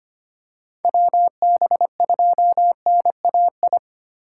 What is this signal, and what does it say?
Morse code, heard as a keyed tone, about 25 wpm, 700 Hz. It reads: WB2NAI
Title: USA Call Signs - 6 Characters - 3x Character Spacing - 25wpm